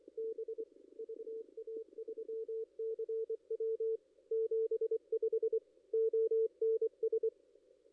Morse code, heard as a keyed tone, about 22 wpm, 410 Hz. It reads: B VA3CW 75 ONS
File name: contests / ss-cw-k5tr-op-n5rz -2300